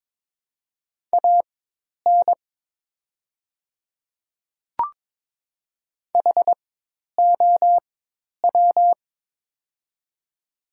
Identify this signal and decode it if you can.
Morse code, heard as a keyed tone, about 22 wpm, spaced as for 10 wpm, 700 Hz. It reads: AN HOW
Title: 4x Character Spacing - Top 100 Words - 22wpm